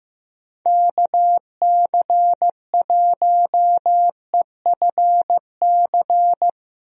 Morse code, heard as a keyed tone, about 15 wpm, 700 Hz. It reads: KC1EFC
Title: USA Call Signs - 6 Characters - Repeated 3x - 15wpm